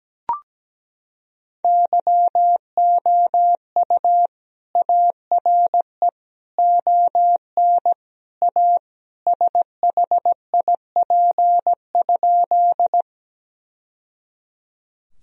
Morse code, heard as a keyed tone, about 17 wpm, 700 Hz. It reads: YOU ARE ON A SHIP?
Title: Sentences from Top 1000 Words - Review Part 2 - 17wpm